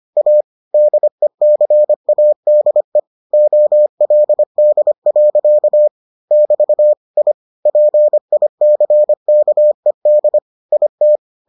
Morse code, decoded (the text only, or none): A DECADE OLD. <BT> I PICKED IT